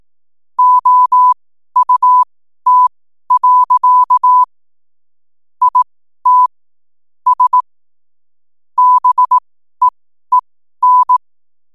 Morse code, heard as a keyed tone, about 18 wpm, spaced as for 12 wpm, 1000 Hz. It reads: OUT. IT S BEEN
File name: n7net / 12wpm